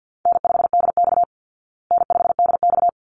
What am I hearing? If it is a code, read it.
Morse code, heard as a keyed tone, about 50 wpm, 700 Hz. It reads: N5DX N5DX
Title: Easy Call Signs Repeated 2x - 50wpm